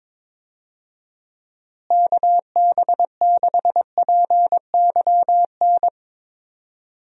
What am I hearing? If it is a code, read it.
Morse code, heard as a keyed tone, about 22 wpm, 700 Hz. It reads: KB6PYN
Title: USA Call Signs - 6 Characters - Repeated 2x - 22wpm